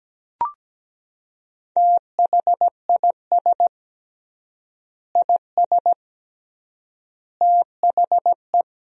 Morse code, heard as a keyed tone, about 17 wpm, 700 Hz. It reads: THIS IS THE
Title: 3x Word-Spacing - Sentences from Top 200 Words - 17wpm